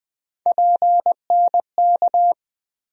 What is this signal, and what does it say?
Morse code, heard as a keyed tone, about 20 wpm, 700 Hz. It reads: PNK
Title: USA Call Sign Postfixes - 3 Letters 20wpm